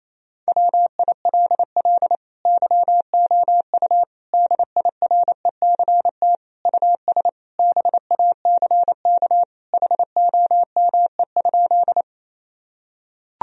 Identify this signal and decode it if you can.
Morse code, heard as a keyed tone, about 27 wpm, 700 Hz. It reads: WILL YOU DIRECT US BACK HOME?
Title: Sentences from Top 1000 Words - Review Part 2 - 28wpm